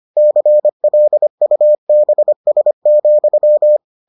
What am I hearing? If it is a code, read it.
Morse code, heard as a keyed tone, about 25 wpm, 600 Hz. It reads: CLUBS,